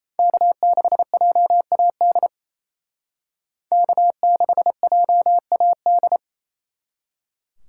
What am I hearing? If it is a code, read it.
Morse code, heard as a keyed tone, about 37 wpm, 700 Hz. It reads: K6JAD K6JAD